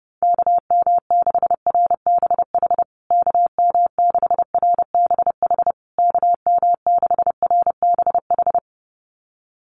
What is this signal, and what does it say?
Morse code, heard as a keyed tone, about 30 wpm, 700 Hz. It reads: KM6RBH KM6RBH KM6RBH
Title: USA Call Signs Repeated 3x 30wpm